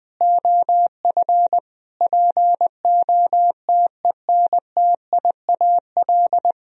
Morse code, heard as a keyed tone, about 21 wpm, 700 Hz. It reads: OF POTENTIAL
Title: Sentences from Top 1000 Words - Review Part 4 - 20wpm